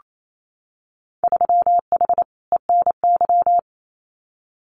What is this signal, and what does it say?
Morse code, heard as a keyed tone, about 28 wpm, 700 Hz. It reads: 3H ENY
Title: Winter Field Day Exchanges Encore 28wpm